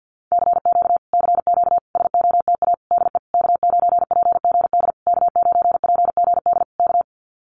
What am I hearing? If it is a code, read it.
Morse code, heard as a keyed tone, about 53 wpm, 700 Hz. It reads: CQ CQ SOTA DE K9PGD K9PGD K
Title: Common QSO Phrases Encore 50wpm